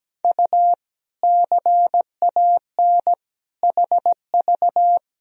Morse code, heard as a keyed tone, about 17 wpm, 700 Hz. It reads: U CAN HV
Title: Sentences from Top 100 Words and Abbreviations 17wpm